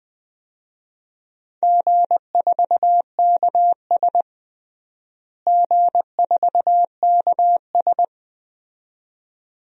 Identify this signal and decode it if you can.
Morse code, heard as a keyed tone, about 21 wpm, 700 Hz. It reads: G4KS G4KS
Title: DXCC-100 Call Signs - 4 Characters - Repeated 2x - 20wpm